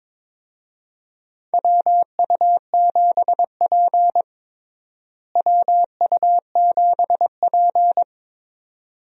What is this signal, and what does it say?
Morse code, heard as a keyed tone, about 22 wpm, 700 Hz. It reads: WU7P WU7P
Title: USA Call Signs - 4 Characters - Repeated 2x - 22wpm